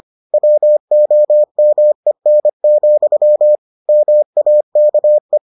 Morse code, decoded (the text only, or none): WOMEN, MAKE